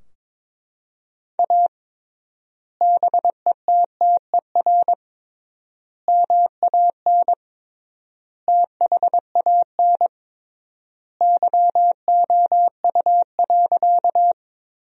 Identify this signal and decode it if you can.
Morse code, heard as a keyed tone, about 21 wpm, 700 Hz. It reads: A BETTER MAN THAN YOU.
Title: 3x Word-Spacing - Sentences from Top 400 Words - 22wpm